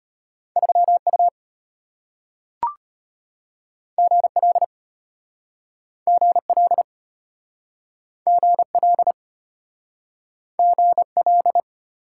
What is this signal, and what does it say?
Morse code, heard as a keyed tone, about 35 wpm, 700 Hz. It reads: 3U GL GL GL GL